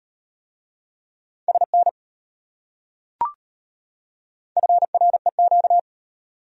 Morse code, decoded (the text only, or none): SN FREQ